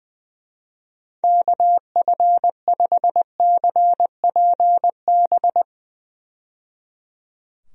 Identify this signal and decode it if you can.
Morse code, heard as a keyed tone, about 20 wpm, 700 Hz. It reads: KF5CPB
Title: USA Call Signs - 6 Characters - 20wpm